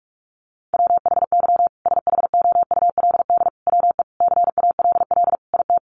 Morse code, heard as a keyed tone, about 44 wpm, 700 Hz. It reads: WHY SHOULD WE CALL IT
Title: Sentences from Top 1100 Words 45wpm